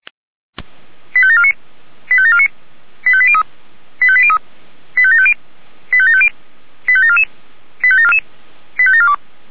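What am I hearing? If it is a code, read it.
Five-tone selective call (ZVEI), heard as quick runs of five tones, tones of about 70 ms. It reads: 85749 85749 85803 85803 85750 85750 85744 85742
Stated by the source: Five-tone ZVEI selective call received by SDR, from audio server of Feuerwehr Orlamünde (fire brigade).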